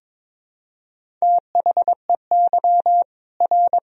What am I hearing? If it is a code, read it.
Morse code, heard as a keyed tone, about 22 wpm, 700 Hz. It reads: THEY R
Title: Sentences from Top 300 Words and Abbreviations 22wpm